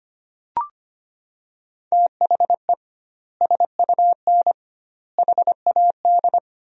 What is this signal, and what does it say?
Morse code, heard as a keyed tone, about 25 wpm, 700 Hz. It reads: THE SUN HAD
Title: 2x Word-Spacing - Sentences from Top 300 Words - 25wpm